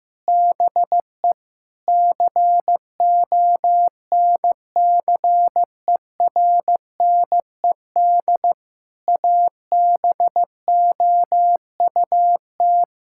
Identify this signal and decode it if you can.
Morse code, heard as a keyed tone, about 15 wpm, 700 Hz. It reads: BE CONCERNED ABOUT
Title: Sentences from Top 1000 Words - Review Part 6 - 15wpm